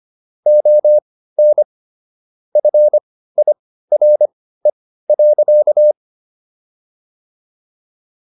Morse code, heard as a keyed tone, about 25 wpm, spaced as for 15 wpm, 600 Hz. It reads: ON FIRE.